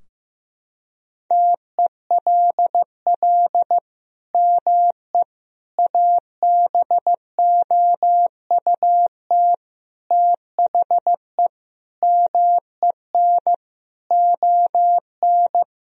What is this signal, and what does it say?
Morse code, heard as a keyed tone, about 15 wpm, 700 Hz. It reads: TELL ME ABOUT THE MEN ON
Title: Sentences from Top 1000 Words - Review Part 2 - 15wpm